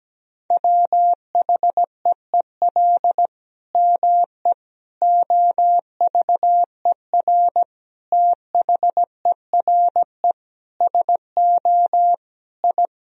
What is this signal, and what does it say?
Morse code, heard as a keyed tone, about 17 wpm, 700 Hz. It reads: WHEEL ME OVER THERE SO I
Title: Sentences from Top 1000 Words - Review Part 3 - 17wpm